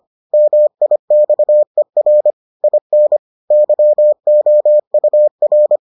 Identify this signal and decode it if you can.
Morse code, heard as a keyed tone, about 25 wpm, 600 Hz. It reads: MIXER IN YOUR